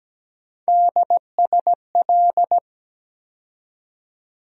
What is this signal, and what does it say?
Morse code, heard as a keyed tone, about 17 wpm, 700 Hz. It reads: DSL
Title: USA Call Sign Postfixes - 3 Letters 17wpm